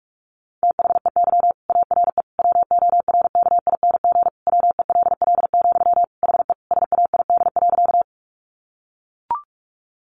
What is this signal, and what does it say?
Morse code, heard as a keyed tone, about 44 wpm, 700 Hz. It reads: THEY ARE WORKING WELL, HE SAID.
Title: Sentences from Top 700 Words 45wpm